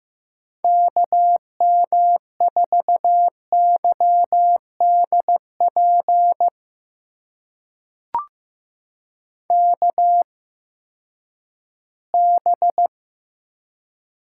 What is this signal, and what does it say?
Morse code, heard as a keyed tone, about 15 wpm, spaced as for 10 wpm, 700 Hz. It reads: KM4YDP KB
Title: USA Call Signs - 6 Characters - 8x Character Spacing - 15wpm